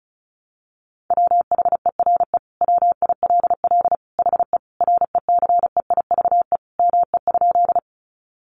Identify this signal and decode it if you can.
Morse code, heard as a keyed tone, about 34 wpm, 700 Hz. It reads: WHERE WILL HE RECEIVE ME?
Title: Sentences from Top 1000 Words 35wpm